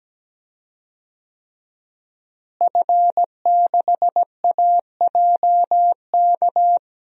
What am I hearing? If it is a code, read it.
Morse code, heard as a keyed tone, about 17 wpm, 700 Hz. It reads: F6AJK
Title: DXCC-100 Call Signs - 5 Characters - Repeated 3x - 17wpm